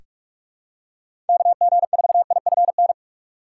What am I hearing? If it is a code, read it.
Morse code, heard as a keyed tone, about 46 wpm, 700 Hz. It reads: KG4IFN
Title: USA Call Signs - 6 Characters - 2x Character Spacing - 45wpm